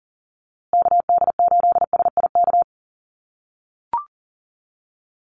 Morse code, heard as a keyed tone, about 40 wpm, 700 Hz. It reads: KD8SIK
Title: USA Call Signs Repeated 3x 40wpm